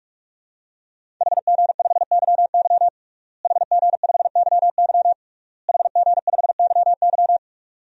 Morse code, heard as a keyed tone, about 46 wpm, 700 Hz. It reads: HG5YY HG5YY HG5YY
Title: DXCC-100 Call Signs - 5 Characters - Repeated 3x - 45wpm